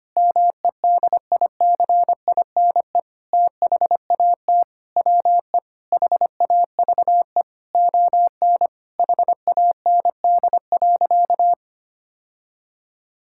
Morse code, heard as a keyed tone, about 25 wpm, 700 Hz. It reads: MEDICINE THAT WE HAVE ON HAND.
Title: Sentences from Top 1000 Words - Review Part 6 - 25wpm